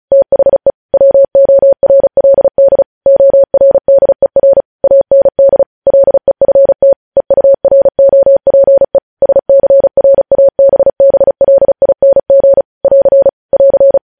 Morse code, heard as a keyed tone, about 36 wpm, 550 Hz. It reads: THE WORLD ORDER AND LEFT EUROPE SCRABBLING <AR> <AR>
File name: podcast / News headlines-1x-35x35